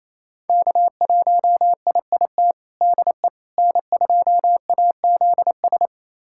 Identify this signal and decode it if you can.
Morse code, heard as a keyed tone, about 28 wpm, 700 Hz. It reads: K1IIT DE N2AZS
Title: USA Callsign DE Callsign - 5 Characters - 28wpm